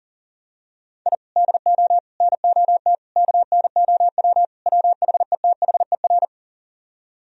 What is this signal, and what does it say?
Morse code, heard as a keyed tone, about 38 wpm, 700 Hz. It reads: I DO NOT KNOW WHETHER
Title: Speed-Racing - Sets of 5 Words 40wpm